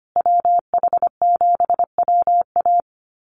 Morse code, heard as a keyed tone, about 25 wpm, 700 Hz. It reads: WH7WA
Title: USA Call Signs Repeated 2x 25wpm